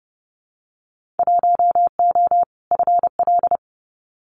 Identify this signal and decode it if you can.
Morse code, heard as a keyed tone, about 30 wpm, 700 Hz. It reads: 1O FL
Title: Winter Field Day Exchanges 30wpm